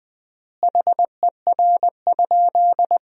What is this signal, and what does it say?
Morse code, heard as a keyed tone, about 20 wpm, 700 Hz. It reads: HER?
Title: 3x Word-Spacing - Sentences from Top 100 Words - 20wpm